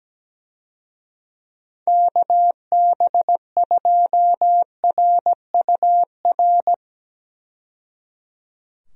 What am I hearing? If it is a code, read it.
Morse code, heard as a keyed tone, about 17 wpm, 700 Hz. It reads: KB2RUR
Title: USA Call Signs - 6 Characters - 17wpm